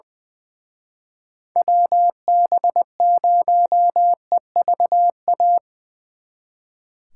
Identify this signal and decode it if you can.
Morse code, heard as a keyed tone, about 24 wpm, 700 Hz. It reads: WB0EVA
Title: Speed Racing - USA 2x3 Call Signs 20wpm